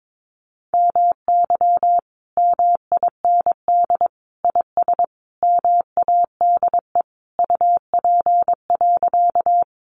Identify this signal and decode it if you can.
Morse code, heard as a keyed tone, about 22 wpm, 700 Hz. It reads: MY MIND IS MADE UP.
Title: Sentences from Top 500 Words 22wpm